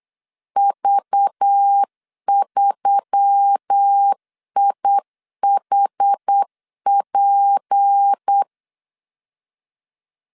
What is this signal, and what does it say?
Morse code, heard as a keyed tone, about 8 wpm, 800 Hz. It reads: V3IHP
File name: CW / DET / 1 6 1500C S B 8wpm 5142